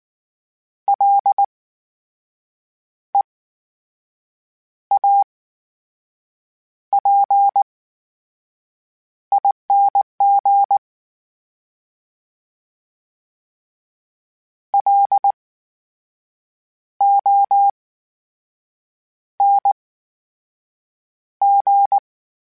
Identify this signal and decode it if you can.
Morse code, heard as a keyed tone, about 19 wpm, spaced as for 6 wpm, 800 Hz. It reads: LEAPING LONG
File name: Special INGs / Special INGs-mp3s / VerySlowToFaster Ing @18